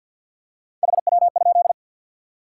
Morse code, heard as a keyed tone, about 53 wpm, 700 Hz. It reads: HW?
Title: Top 180 QSO Elements 50wpm